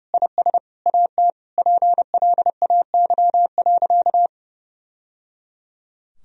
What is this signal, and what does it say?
Morse code, heard as a keyed tone, about 30 wpm, 700 Hz. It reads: IS AT PLAY.
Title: Sentences from Top 1000 Words - Review Part 4 - 30wpm